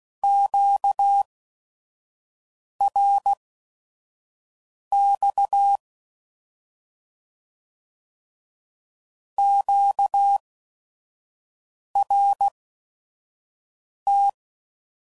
Morse code, heard as a keyed tone, about 16 wpm, spaced as for 5 wpm, 800 Hz. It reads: QRX QRT